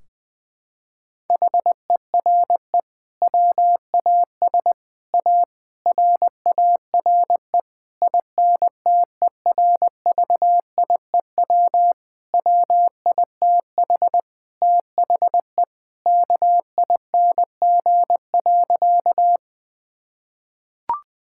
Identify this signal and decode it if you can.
Morse code, heard as a keyed tone, about 20 wpm, 700 Hz. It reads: HERE WAS A RARE INTERVIEW WITH THE KING.
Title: Sentences from Top 1600 Words 20wpm